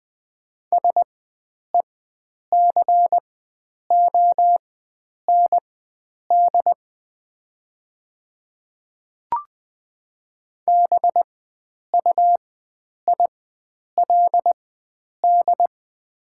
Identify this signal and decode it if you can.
Morse code, heard as a keyed tone, about 20 wpm, spaced as for 9 wpm, 700 Hz. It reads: SECOND BUILD
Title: 4x Character Spacing - Top 300 Words - 20wpm